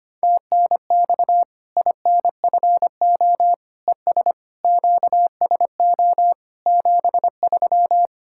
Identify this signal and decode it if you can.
Morse code, heard as a keyed tone, about 25 wpm, 700 Hz. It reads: TNX INFO ES QSO 73
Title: K1USN SST QSO Practice 25wpm